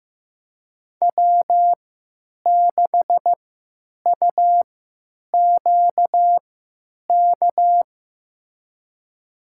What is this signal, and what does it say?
Morse code, heard as a keyed tone, about 15 wpm, spaced as for 9 wpm, 700 Hz. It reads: W6UQK
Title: USA Call Signs - 5 Characters - 3x Character Spacing - 15wpm